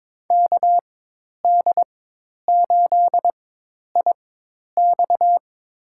Morse code, heard as a keyed tone, about 22 wpm, spaced as for 10 wpm, 700 Hz. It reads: KD8IX
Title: USA Call Signs - 5 Characters - 4x Character Spacing - 22wpm